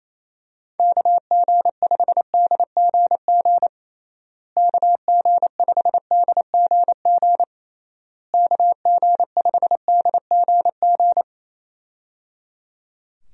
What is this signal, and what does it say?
Morse code, heard as a keyed tone, about 28 wpm, 700 Hz. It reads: KG5DGG KG5DGG KG5DGG
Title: USA Call Signs - 6 Characters - Repeated 3x - 28wpm